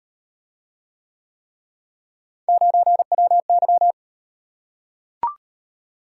Morse code, heard as a keyed tone, about 38 wpm, 700 Hz. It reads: 9WY